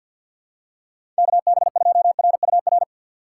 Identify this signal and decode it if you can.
Morse code, heard as a keyed tone, about 53 wpm, 700 Hz. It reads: KB2RUR
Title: USA Call Signs - 6 Characters - 8x Character Spacing - 50wpm